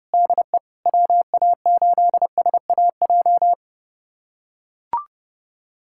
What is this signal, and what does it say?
Morse code, heard as a keyed tone, about 30 wpm, 700 Hz. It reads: DE WA8SAJ
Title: USA Callsign DE Callsign - 6 Characters - 30wpm